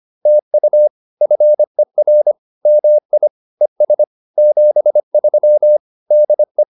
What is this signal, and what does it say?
Morse code, heard as a keyed tone, about 25 wpm, 600 Hz. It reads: TU FER MI ES 73 DE